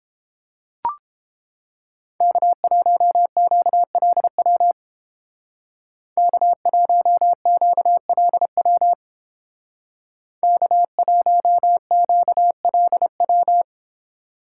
Morse code, heard as a keyed tone, about 31 wpm, 700 Hz. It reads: K1QLW K1QLW K1QLW